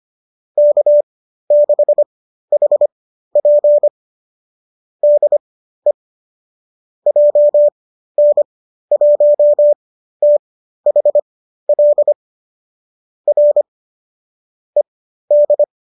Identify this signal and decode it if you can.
Morse code, heard as a keyed tone, about 25 wpm, spaced as for 13 wpm, 590 Hz. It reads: K6HP DE JN1THL R ED